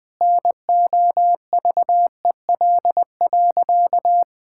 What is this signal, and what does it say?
Morse code, heard as a keyed tone, about 20 wpm, 700 Hz. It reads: NOVEL.
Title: Sentences with 4-Word Phrases - 20wpm